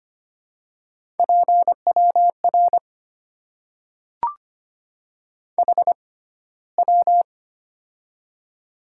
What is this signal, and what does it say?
Morse code, heard as a keyed tone, about 25 wpm, spaced as for 15 wpm, 700 Hz. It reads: PWR HW
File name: Top 90 QSO Elements - 6x Character Spacing - 25wpm